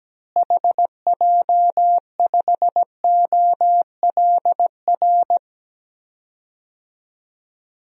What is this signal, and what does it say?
Morse code, heard as a keyed tone, about 17 wpm, 700 Hz. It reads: HJ5OLR
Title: DXCC-100 Call Signs - 6 Characters - Repeated 3x - 17wpm